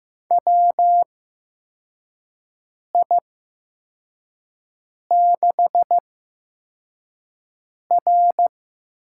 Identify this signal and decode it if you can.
Morse code, heard as a keyed tone, about 15 wpm, spaced as for 4 wpm, 700 Hz. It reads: WI6R
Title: USA Call Signs - 4 Characters - 8x Character Spacing - 15wpm